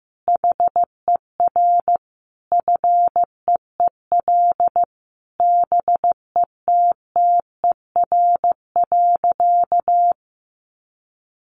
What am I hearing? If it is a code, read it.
Morse code, heard as a keyed tone, about 15 wpm, 700 Hz. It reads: HER FEEL BETTER.
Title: Easy Phrases 15wpm